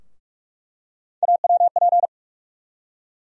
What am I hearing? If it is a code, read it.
Morse code, heard as a keyed tone, about 46 wpm, 700 Hz. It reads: AWP
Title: USA Call Sign Postfixes - 3 Letters 45wpm